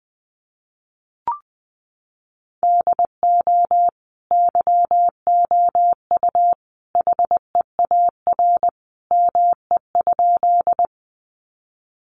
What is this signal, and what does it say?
Morse code, heard as a keyed tone, about 20 wpm, 700 Hz. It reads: DO YOU HEAR ME?
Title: Sentences from Top 300 Words 20wpm